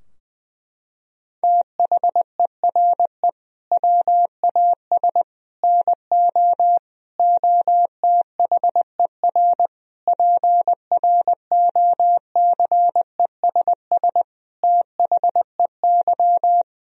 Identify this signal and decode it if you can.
Morse code, heard as a keyed tone, about 21 wpm, 700 Hz. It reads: THERE WAS NO OTHER PROCESS THEY
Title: Sentences from Top 1000 Words - Review Part 3 - 20wpm